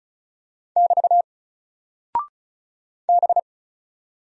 Morse code, heard as a keyed tone, about 36 wpm, 700 Hz. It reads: <BT> B
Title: Single Character (Letter:Number:Punctuation:Prosign) Alt 35wpm